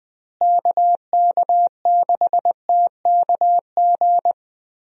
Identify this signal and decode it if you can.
Morse code, heard as a keyed tone, about 20 wpm, 700 Hz. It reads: KK6TKG
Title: USA Call Signs - 6 Characters - Repeated 3x - 20wpm